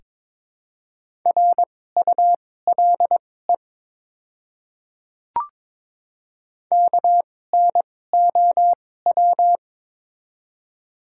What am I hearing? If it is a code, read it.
Morse code, heard as a keyed tone, about 22 wpm, spaced as for 16 wpm, 700 Hz. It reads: RULE KNOW
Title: 2x Character Spacing - Top 500 Words - 22wpm